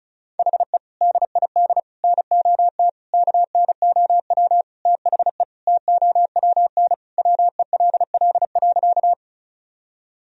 Morse code, heard as a keyed tone, about 34 wpm, 700 Hz. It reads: HE DID NOT KNOW THE TOWN WELL.
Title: Sentences from Top 1000 Words - Review Part 2 - 35wpm